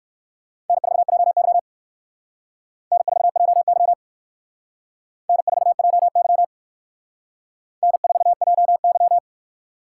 Morse code, heard as a keyed tone, about 58 wpm, 700 Hz. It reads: N4JY N4JY N4JY N4JY